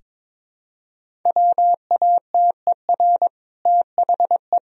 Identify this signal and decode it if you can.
Morse code, heard as a keyed tone, about 22 wpm, 700 Hz. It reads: WATER THE
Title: Sentences from Top 1000 Words - Review Part 1 - 22wpm